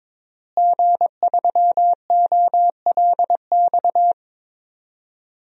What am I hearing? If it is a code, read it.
Morse code, heard as a keyed tone, about 22 wpm, 700 Hz. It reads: G3OLX
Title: DXCC-100 Call Signs - 5 Characters - Repeated 3x - 22wpm